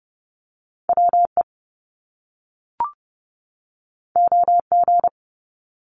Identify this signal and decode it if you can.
Morse code, heard as a keyed tone, about 30 wpm, 700 Hz. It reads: WE OG
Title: Top Two Letter Combinations in Words 30wpm